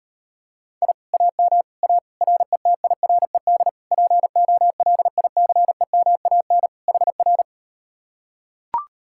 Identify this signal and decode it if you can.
Morse code, heard as a keyed tone, about 38 wpm, 700 Hz. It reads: I AM A RETIRED POLICEMAN HR